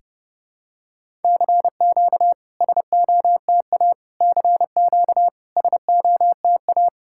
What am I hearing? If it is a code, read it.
Morse code, heard as a keyed tone, about 32 wpm, 700 Hz. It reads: CQ SOTA CQ SOTA
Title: SOTA QSO Practice 30wpm